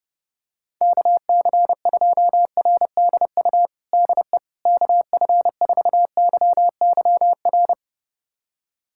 Morse code, heard as a keyed tone, about 32 wpm, 700 Hz. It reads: KC2RDU DE KF4YYR
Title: USA Callsign DE Callsign 30wpm